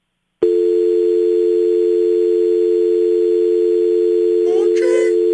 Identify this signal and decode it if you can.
Dial tone (North America), 350 + 440 Hz.